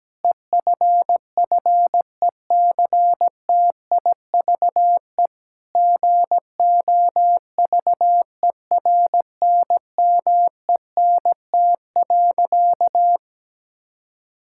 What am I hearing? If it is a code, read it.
Morse code, heard as a keyed tone, about 17 wpm, 700 Hz. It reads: EFFECTIVE GOVERNMENT.
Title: Sentences from Top 1000 Words - Review Part 6 - 17wpm